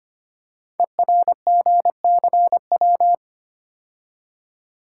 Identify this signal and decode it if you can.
Morse code, heard as a keyed tone, about 25 wpm, 700 Hz. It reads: ERGCW
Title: Code Groups - 5 Letter:Numbers - 25wpm